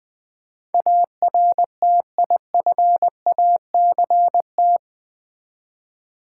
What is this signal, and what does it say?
Morse code, heard as a keyed tone, about 20 wpm, 690 Hz. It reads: ARTIFACT
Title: Easy Words with Hints - 20wpm